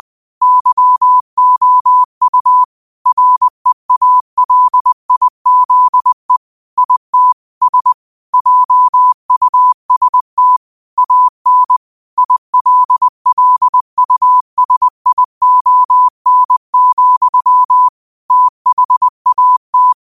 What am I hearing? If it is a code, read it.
Morse code, heard as a keyed tone, about 20 wpm, 1000 Hz. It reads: YOU REALIZE IT S JUST AN ILLUSION, THAT